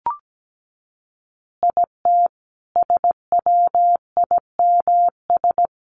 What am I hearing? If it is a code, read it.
Morse code, heard as a keyed tone, about 17 wpm, 700 Hz. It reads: IT SWIMS
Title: Easy Phrases 17wpm